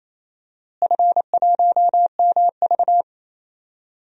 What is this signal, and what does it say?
Morse code, heard as a keyed tone, about 28 wpm, 700 Hz. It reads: F1MV
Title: DXCC-100 Call Signs - 4 Characters - Repeated 2x - 28wpm